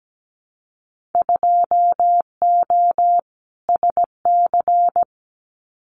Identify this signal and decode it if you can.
Morse code, heard as a keyed tone, about 17 wpm, 700 Hz. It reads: 2O SC
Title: Winter Field Day Exchanges Encore 17wpm